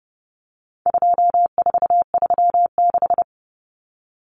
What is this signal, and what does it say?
Morse code, heard as a keyed tone, about 30 wpm, 700 Hz. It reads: 2436
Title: Numbers - 4-Digit 30wpm